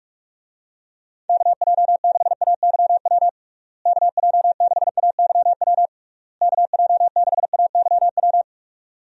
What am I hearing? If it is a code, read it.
Morse code, heard as a keyed tone, about 42 wpm, 700 Hz. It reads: KJ6AYW KJ6AYW KJ6AYW
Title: USA Call Signs - 6 Characters - Repeated 3x - 45wpm